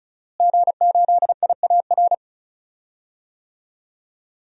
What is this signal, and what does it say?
Morse code, heard as a keyed tone, about 35 wpm, 700 Hz. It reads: G8IAR
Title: DXCC-100 Call Signs - 5 Characters - Repeated 2x - 35wpm